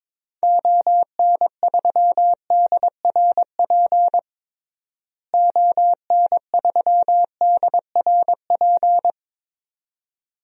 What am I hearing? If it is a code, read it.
Morse code, heard as a keyed tone, about 21 wpm, 700 Hz. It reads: ON3DRP ON3DRP
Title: DXCC-100 Call Signs - 6 Characters - Repeated 2x - 22wpm